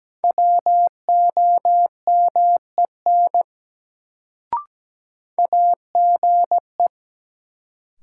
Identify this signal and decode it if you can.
Morse code, heard as a keyed tone, about 17 wpm, 700 Hz. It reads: WOMEN AGE
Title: Sets of 3 Words - New Word Drill - 17wpm